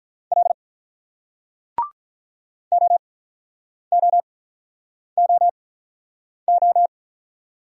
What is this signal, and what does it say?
Morse code, heard as a keyed tone, about 46 wpm, 700 Hz. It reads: L O O O O